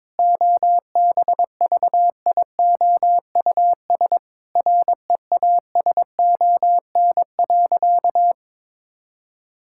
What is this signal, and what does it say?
Morse code, heard as a keyed tone, about 22 wpm, 700 Hz. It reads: OBVIOUS REASON.
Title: Sentences from Top 1600 Words 22wpm